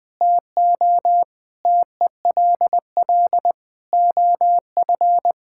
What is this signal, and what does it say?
Morse code, heard as a keyed tone, about 20 wpm, 700 Hz. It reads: TO TELL OF
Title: Sentences from Top 1600 Words 20wpm